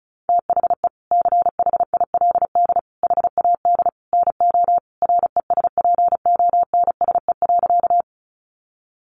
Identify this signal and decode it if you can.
Morse code, heard as a keyed tone, about 34 wpm, 700 Hz. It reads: THE CHILD HAD NO RESPONSE.
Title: Sentences from Top 1400 Words 35wpm